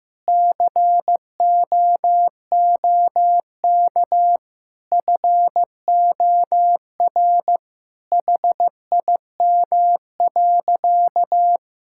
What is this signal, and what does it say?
Morse code, heard as a keyed tone, about 15 wpm, 700 Hz. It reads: COOK FOR HIM.
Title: Sentences from Top 1600 Words 15wpm